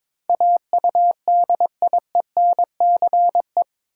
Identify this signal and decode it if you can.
Morse code, heard as a keyed tone, about 22 wpm, 700 Hz. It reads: AUDIENCE
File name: practice / 22wpm / Eight Letter Words - 22wpm